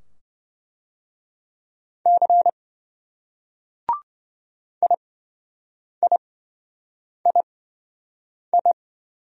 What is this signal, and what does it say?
Morse code, heard as a keyed tone, about 27 wpm, spaced as for 7 wpm, 700 Hz. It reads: C IIII